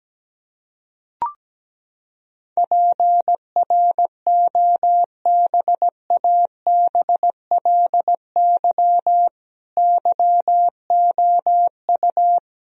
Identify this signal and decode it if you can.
Morse code, heard as a keyed tone, about 17 wpm, 700 Hz. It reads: PROBABLY YOU
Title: Sentences from Top 1000 Words - Review Part 4 - 17wpm